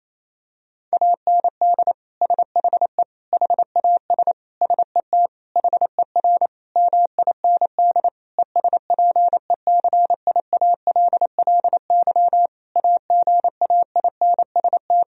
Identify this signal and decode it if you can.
Morse code, heard as a keyed tone, about 28 wpm, 700 Hz. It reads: AND SHE HAS SET HER MIND ESPECIALLY AGAINST
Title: Sentences from Top 1000 Words - Review Part 4 - 28wpm